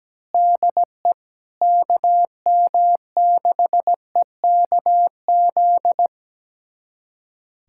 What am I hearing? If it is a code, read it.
Morse code, heard as a keyed tone, about 17 wpm, 700 Hz. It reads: DE KM6EKZ
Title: USA Callsign DE Callsign 17wpm